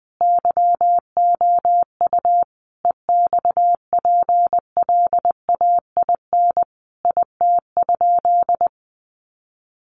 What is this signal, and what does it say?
Morse code, heard as a keyed tone, about 20 wpm, 700 Hz. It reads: YOU EXPLAIN IT?
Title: Sentences from Top 1100 Words 20wpm